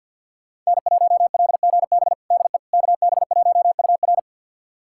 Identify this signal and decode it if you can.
Morse code, heard as a keyed tone, about 50 wpm, 700 Hz. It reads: N1LGB DE KB1UR
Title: USA Callsign DE Callsign - 5 Characters - 50wpm